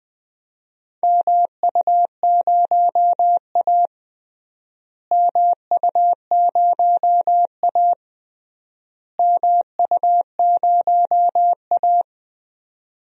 Morse code, heard as a keyed tone, about 20 wpm, 700 Hz. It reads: MU0A MU0A MU0A
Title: DXCC-100 Call Signs - 4 Characters - Repeated 3x - 20wpm